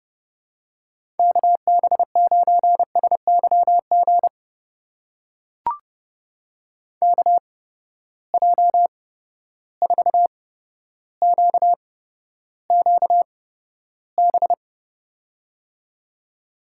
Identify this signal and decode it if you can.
Morse code, heard as a keyed tone, about 30 wpm, spaced as for 13 wpm, 700 Hz. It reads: KB9SYG KJ4QQB
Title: USA Call Signs - 6 Characters - 8x Character Spacing - 30wpm